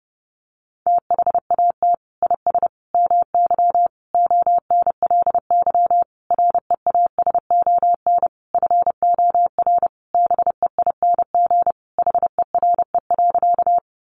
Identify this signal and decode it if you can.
Morse code, heard as a keyed tone, about 30 wpm, 700 Hz. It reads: THAT IS MY ONLY REASON FOR BEING HERE.
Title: Sentences from Top 700 Words 30wpm